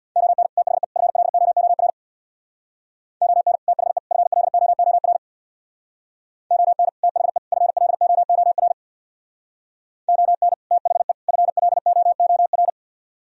Morse code, heard as a keyed tone, about 61 wpm, 700 Hz. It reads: ON THE FLOOR ON THE FLOOR ON THE FLOOR ON THE FLOOR
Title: Speed-Racing - Sets of 3 Words 50wpm